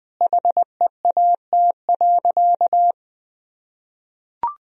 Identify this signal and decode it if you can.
Morse code, heard as a keyed tone, about 20 wpm, 700 Hz. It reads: HEAT.
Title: Sentences from Top 1000 Words - Review Part 3 - 20wpm